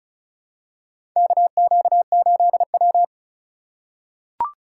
Morse code, heard as a keyed tone, about 34 wpm, 700 Hz. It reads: KQ8W
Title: USA Call Signs - 4 Characters - 6x Character Spacing - 35wpm